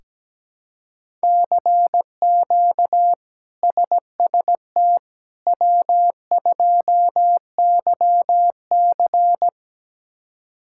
Morse code, heard as a keyed tone, about 17 wpm, 700 Hz. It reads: CQ SST W2YC
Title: K1USN SST QSO Practice - 17wpm